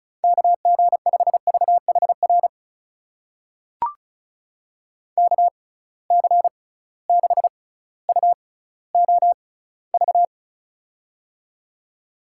Morse code, heard as a keyed tone, about 36 wpm, spaced as for 18 wpm, 700 Hz. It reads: KG5VHR KC6UOV
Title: USA Call Signs - 6 Characters - 6x Character Spacing - 35wpm